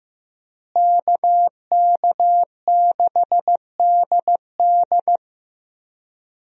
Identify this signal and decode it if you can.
Morse code, heard as a keyed tone, about 15 wpm, 700 Hz. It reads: KK6DD
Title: USA Call Signs - 5 Characters - Repeated 2x - 15wpm